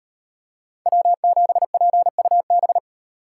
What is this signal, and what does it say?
Morse code, heard as a keyed tone, about 38 wpm, 700 Hz. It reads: W7PUB